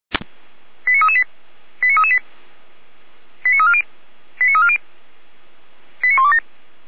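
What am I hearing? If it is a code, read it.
Five-tone selective call (ZVEI), heard as quick runs of five tones, tones of about 70 ms. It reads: 89338 89338 89340 89340 89127
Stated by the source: ZVEI five-tone call 89336 received on SDR.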